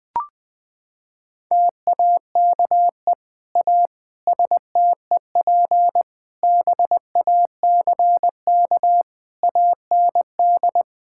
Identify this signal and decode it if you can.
Morse code, heard as a keyed tone, about 20 wpm, 700 Hz. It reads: TAKE A STEP BACK AND
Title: Sets of 8 Words - 20wpm